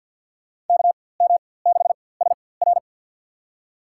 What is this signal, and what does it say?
Morse code, heard as a keyed tone, about 47 wpm, spaced as for 23 wpm, 700 Hz. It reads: KM6SR
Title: USA Call Signs - 5 Characters - 4x Character Spacing - 50wpm